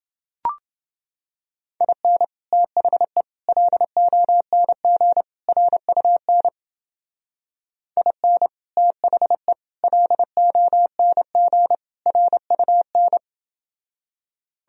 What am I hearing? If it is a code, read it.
Morse code, heard as a keyed tone, about 29 wpm, 700 Hz. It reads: IN THE LONG RUN IN THE LONG RUN